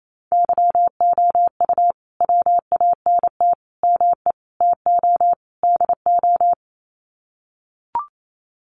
Morse code, heard as a keyed tone, about 28 wpm, 700 Hz. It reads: YOU WANT ME TO DO
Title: Sets of 5 Words Encore2 28wpm